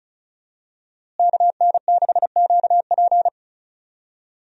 Morse code, heard as a keyed tone, about 34 wpm, 700 Hz. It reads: KN6QP
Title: USA Call Signs - 5 Characters - Repeated 3x - 35wpm